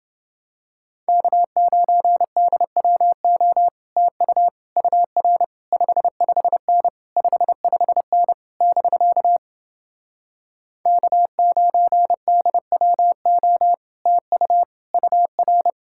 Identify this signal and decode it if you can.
Morse code, heard as a keyed tone, about 30 wpm, 700 Hz. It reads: K9DWO TU UR 55N 55N <BK> K9DWO TU UR